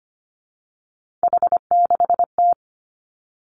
Morse code, heard as a keyed tone, about 25 wpm, 700 Hz. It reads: H6T
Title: Three Challenging Letter-Number Combinations 25wpm